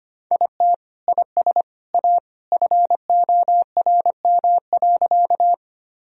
Morse code, heard as a keyed tone, about 25 wpm, 700 Hz. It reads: IT IS A FORM.
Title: Sentences from Top 1000 Words - Review Part 1 - 25wpm